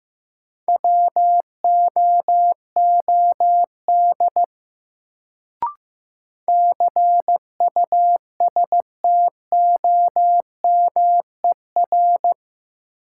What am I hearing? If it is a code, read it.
Morse code, heard as a keyed tone, about 15 wpm, 700 Hz. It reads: WOOD CUSTOMER
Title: Sentences from Top 1100 Words - New Word Drill Alt - 15wpm